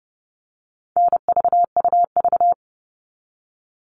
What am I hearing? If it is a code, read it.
Morse code, heard as a keyed tone, about 30 wpm, 700 Hz. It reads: NVUV
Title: All Cut Numbers - 4-Digit 30wpm